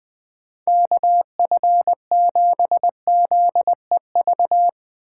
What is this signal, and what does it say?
Morse code, heard as a keyed tone, about 21 wpm, 700 Hz. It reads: KF7ZEV